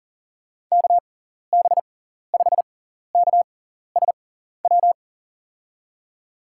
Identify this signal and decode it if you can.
Morse code, heard as a keyed tone, about 40 wpm, spaced as for 14 wpm, 700 Hz. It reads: KB5KSW
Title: USA Call Signs - 6 Characters - 6x Character Spacing - 40wpm